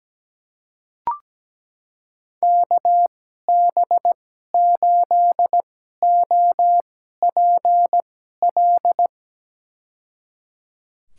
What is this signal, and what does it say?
Morse code, heard as a keyed tone, about 17 wpm, spaced as for 12 wpm, 700 Hz. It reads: KB8OPL
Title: USA Call Signs - 6 Characters - 2x Character Spacing - 17wpm